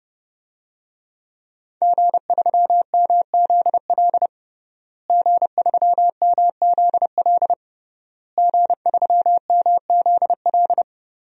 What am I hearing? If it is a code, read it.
Morse code, heard as a keyed tone, about 30 wpm, 700 Hz. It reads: G3MZL G3MZL G3MZL
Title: DXCC-100 Call Signs - 5 Characters - Repeated 3x - 30wpm